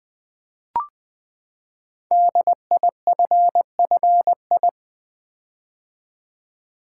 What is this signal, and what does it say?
Morse code, heard as a keyed tone, about 20 wpm, 700 Hz. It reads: DIFFI
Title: Build A Word - 5 Letters At A Time - 10-Letter Words - 20wpm